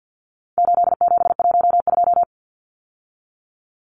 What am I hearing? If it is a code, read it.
Morse code, heard as a keyed tone, about 50 wpm, 700 Hz. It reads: 8712
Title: Numbers - 4-Digit 50wpm